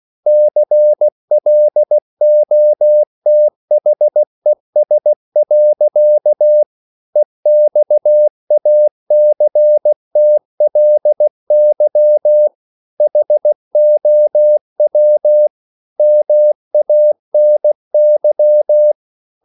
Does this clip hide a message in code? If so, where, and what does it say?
Yes: Morse code from 0.3 to 18.9 s, heard as a keyed tone, about 16 wpm, 600 Hz. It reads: CLOTHES. EXACTLY HOW MANY